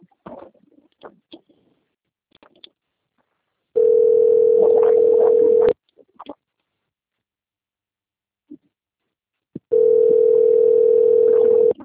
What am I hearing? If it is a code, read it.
Ringback tone (North America), 440 + 480 Hz.